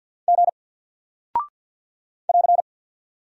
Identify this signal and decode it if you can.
Morse code, heard as a keyed tone, about 48 wpm, 700 Hz. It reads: G <AR>
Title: Single Character (Letter:Number:Punctuation:Prosign) Alt 50wpm